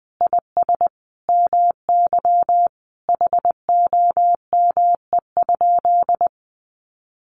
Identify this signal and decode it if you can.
Morse code, heard as a keyed tone, about 20 wpm, 700 Hz. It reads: IS MY HOME?
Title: Sentences from Top 1200 Words 20wpm